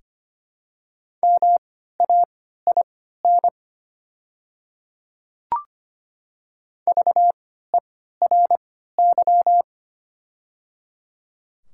Morse code, heard as a keyed tone, about 25 wpm, spaced as for 14 wpm, 700 Hz. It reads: MAIN VERY
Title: 3x Character Spacing - Top 300 Words - 25wpm